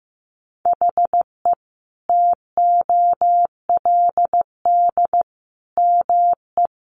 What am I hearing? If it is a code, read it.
Morse code, heard as a keyed tone, about 15 wpm, 700 Hz. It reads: HE TOLD ME
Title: Sentences from Top 1300 Words 15wpm